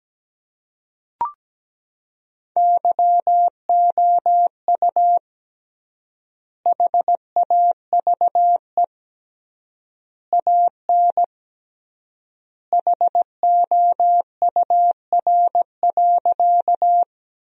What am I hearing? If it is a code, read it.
Morse code, heard as a keyed tone, about 17 wpm, 700 Hz. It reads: YOU HAVE AN HOUR.
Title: 3x Word-Spacing - Sentences from Top 400 Words - 17wpm